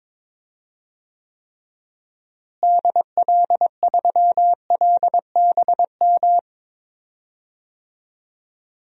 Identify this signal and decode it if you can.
Morse code, heard as a keyed tone, about 22 wpm, 700 Hz. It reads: DL3LBM
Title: DXCC-100 Call Signs - 6 Characters - Repeated 3x - 22wpm